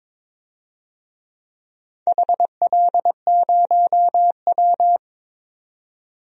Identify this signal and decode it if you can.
Morse code, heard as a keyed tone, about 22 wpm, 700 Hz. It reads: HL0W
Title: DXCC-100 Call Signs - 4 Characters - 22wpm